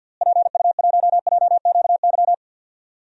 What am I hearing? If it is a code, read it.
Morse code, heard as a keyed tone, about 53 wpm, 700 Hz. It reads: PU1JQY
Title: DXCC-100 Call Signs - 6 Characters - Repeated 3x - 50wpm